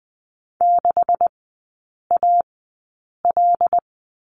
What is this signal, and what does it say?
Morse code, heard as a keyed tone, about 20 wpm, spaced as for 8 wpm, 700 Hz. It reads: 6AL
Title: 198 Lesson 32e - Letter k - Three Characters 20wpm